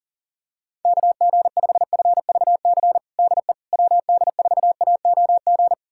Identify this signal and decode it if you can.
Morse code, heard as a keyed tone, about 40 wpm, 700 Hz. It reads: KG5FVC DE WD4AOG
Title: USA Callsign DE Callsign 40wpm